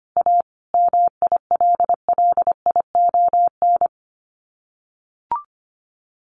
Morse code, heard as a keyed tone, about 25 wpm, 700 Hz. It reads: A MILLION
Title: Sets of 5 Words Encore2 25wpm